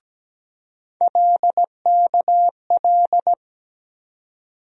Morse code, heard as a keyed tone, about 17 wpm, 700 Hz. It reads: LKL
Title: USA Call Sign Postfixes - 3 Letters 17wpm